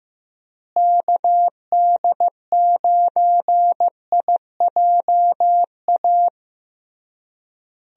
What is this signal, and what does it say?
Morse code, heard as a keyed tone, about 15 wpm, 700 Hz. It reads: KD9IJA
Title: USA Call Signs - 6 Characters - Repeated 3x - 15wpm